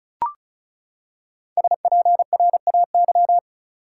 Morse code, heard as a keyed tone, about 34 wpm, 700 Hz. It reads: SPRAY
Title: Five Letter Words - 35wpm